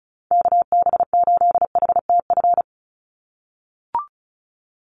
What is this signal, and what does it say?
Morse code, heard as a keyed tone, about 34 wpm, 700 Hz. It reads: KB8HTF
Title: USA Call Signs 35wpm